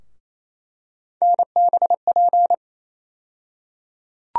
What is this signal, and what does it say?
Morse code, heard as a keyed tone, about 28 wpm, 700 Hz. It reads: NBP
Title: USA Call Sign Postfixes - 3 Letters 28wpm